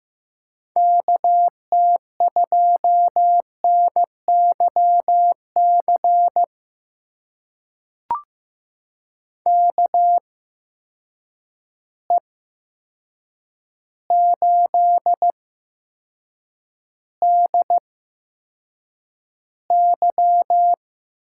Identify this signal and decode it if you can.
Morse code, heard as a keyed tone, about 15 wpm, spaced as for 7 wpm, 700 Hz. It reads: KT2NYC KE8DY
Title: USA Call Signs - 6 Characters - 8x Character Spacing - 15wpm